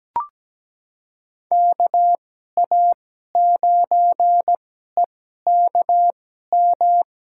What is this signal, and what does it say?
Morse code, heard as a keyed tone, about 17 wpm, spaced as for 12 wpm, 700 Hz. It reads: KA9EKM
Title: USA Call Signs - 2x Character Spacing - 17wpm